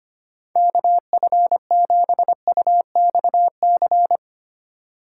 Morse code, heard as a keyed tone, about 25 wpm, 700 Hz. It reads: KF7UXC
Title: K1USN SST QSO Practice - 25wpm